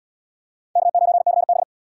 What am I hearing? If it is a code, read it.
Morse code, heard as a keyed tone, about 78 wpm, 700 Hz. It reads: K9CB